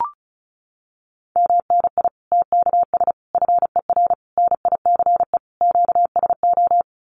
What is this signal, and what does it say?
Morse code, heard as a keyed tone, about 34 wpm, 700 Hz. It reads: MNI TKS FER NICE QSO
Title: Common QSO Phrases 35wpm